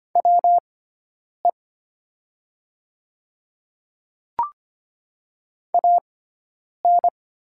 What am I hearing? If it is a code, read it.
Morse code, heard as a keyed tone, about 25 wpm, spaced as for 9 wpm, 700 Hz. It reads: WE AN
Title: 6x Character Spacing - Top 100 Words - 25wpm